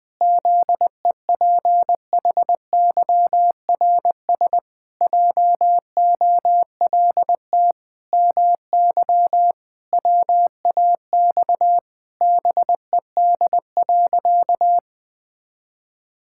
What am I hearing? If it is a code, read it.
Morse code, heard as a keyed tone, about 21 wpm, 700 Hz. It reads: ZEPHYRS JOLT MY WAX BED.
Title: Pangram Challenge v2 - 20wpm